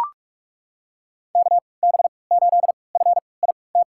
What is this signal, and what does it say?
Morse code, heard as a keyed tone, about 42 wpm, spaced as for 25 wpm, 690 Hz. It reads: KB8FIT
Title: USA Call Signs - 6 Characters - 3x Character Spacing - 45wpm